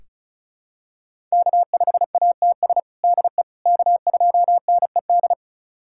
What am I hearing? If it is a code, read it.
Morse code, heard as a keyed tone, about 36 wpm, 700 Hz. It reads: K5ATS DE K2NED
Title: USA Callsign DE Callsign - 5 Characters - 35wpm